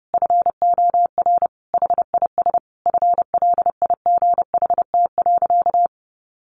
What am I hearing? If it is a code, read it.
Morse code, heard as a keyed tone, about 30 wpm, 700 Hz. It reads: FOR HIS FLIGHT.
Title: Sentences from Top 1400 Words 30wpm